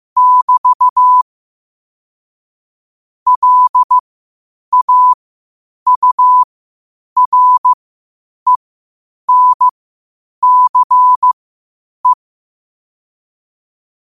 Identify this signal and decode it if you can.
Morse code, heard as a keyed tone, about 15 wpm, spaced as for 8 wpm, 1000 Hz. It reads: <BT> LAURENCE
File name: qotdfiles / 8wpm